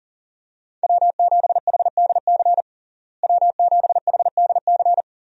Morse code, heard as a keyed tone, about 40 wpm, 700 Hz. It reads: W7HDC W7HDC
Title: USA Call Signs - 5 Characters - Repeated 2x - 40wpm